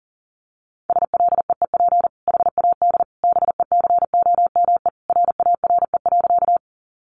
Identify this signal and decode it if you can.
Morse code, heard as a keyed tone, about 40 wpm, 700 Hz. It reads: SLEEP HAD BECOME RARE.
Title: Sentences from Top 1100 Words 40wpm